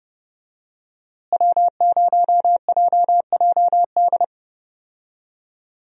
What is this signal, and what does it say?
Morse code, heard as a keyed tone, about 30 wpm, 700 Hz. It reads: W0JJD
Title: USA Call Signs - 5 Characters - Repeated 3x - 30wpm